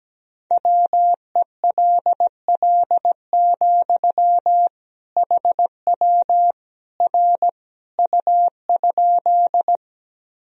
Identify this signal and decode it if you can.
Morse code, heard as a keyed tone, about 17 wpm, 700 Hz. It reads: WELL, HW R U?
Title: Sentences from Top 200 Words and Abbreviations 17wpm